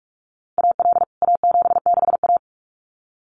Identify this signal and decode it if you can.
Morse code, heard as a keyed tone, about 44 wpm, 700 Hz. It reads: AL A76A
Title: CWT Exchange 45wpm